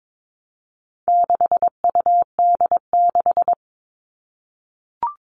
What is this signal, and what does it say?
Morse code, heard as a keyed tone, about 22 wpm, 700 Hz. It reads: 6UD6
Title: All Cut Numbers - 4-Digit 22wpm